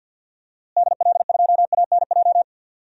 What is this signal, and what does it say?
Morse code, heard as a keyed tone, about 49 wpm, 700 Hz. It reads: DL2ANJ
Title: DXCC-100 Call Signs - 6 Characters - Repeated 3x - 50wpm